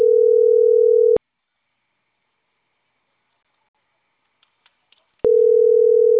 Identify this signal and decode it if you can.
Ringback tone (North America), 440 + 480 Hz.
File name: CENTRAL CAEA / out agcaea02